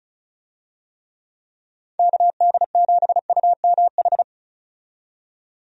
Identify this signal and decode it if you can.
Morse code, heard as a keyed tone, about 34 wpm, 700 Hz. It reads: KD7UMH
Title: USA Call Signs - 6 Characters - 35wpm